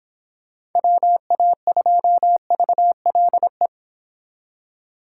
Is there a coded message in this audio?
Yes: Morse code from 0.8 to 3.7 s, heard as a keyed tone, about 25 wpm, 700 Hz. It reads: WA2VLE